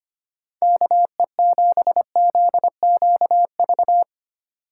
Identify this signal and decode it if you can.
Morse code, heard as a keyed tone, about 25 wpm, 700 Hz. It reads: KE7ZQV
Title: USA Call Signs - 2x Character Spacing - 25wpm